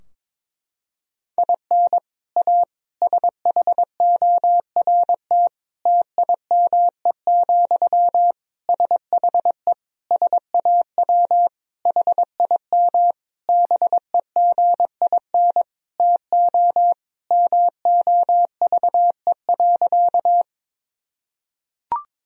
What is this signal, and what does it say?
Morse code, heard as a keyed tone, about 21 wpm, 700 Hz. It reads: IN A SHORT TIME, SHE SAW HIM BEGIN TO MOVE.
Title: Sentences from Top 1000 Words - Review Part 2 - 22wpm